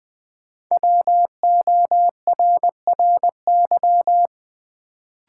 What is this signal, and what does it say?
Morse code, heard as a keyed tone, about 20 wpm, 700 Hz. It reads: WORRY
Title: Sentences from Top 1100 Words - New Word Drill - 20wpm